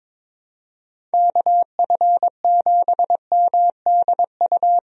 Morse code, heard as a keyed tone, about 22 wpm, 700 Hz. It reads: KF7MDU
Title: USA Call Signs - 6 Characters - 22wpm